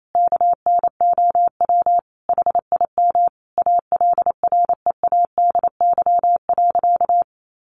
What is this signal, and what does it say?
Morse code, heard as a keyed tone, about 28 wpm, 700 Hz. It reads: KNOW HIM ALREADY.
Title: Sentences from Top 1000 Words 28wpm